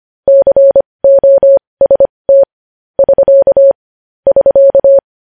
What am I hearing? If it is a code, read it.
Morse code, heard as a keyed tone, about 25 wpm, spaced as for 20 wpm, 550 Hz. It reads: COST <SK> <SK>
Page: News Headlines in Morse Code at 20 WPM